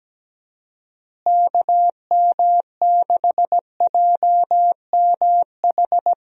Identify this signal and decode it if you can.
Morse code, heard as a keyed tone, about 17 wpm, 700 Hz. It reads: KM6JMH
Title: USA Call Signs - 6 Characters - Repeated 3x - 17wpm